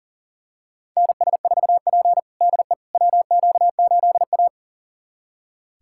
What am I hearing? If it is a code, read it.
Morse code, heard as a keyed tone, about 40 wpm, 700 Hz. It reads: NS4P DE WQ8A
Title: USA Callsign DE Callsign - 4 Characters - 40wpm